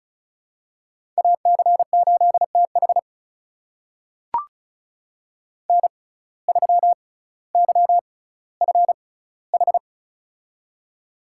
Morse code, heard as a keyed tone, about 36 wpm, spaced as for 18 wpm, 700 Hz. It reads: AC8TH N3YFH
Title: USA Call Signs - 6x Character Spacing - 35wpm